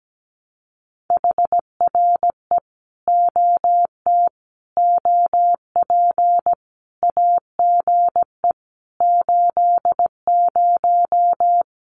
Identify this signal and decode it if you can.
Morse code, heard as a keyed tone, about 17 wpm, 700 Hz. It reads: HRE OT OP AGE 80
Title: Common QSO Phrases 17wpm